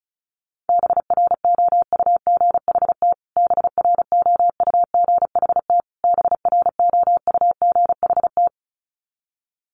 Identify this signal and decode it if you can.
Morse code, heard as a keyed tone, about 34 wpm, 700 Hz. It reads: BROUGHT BROUGHT BROUGHT
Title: Top 500 Words Repeated 35wpm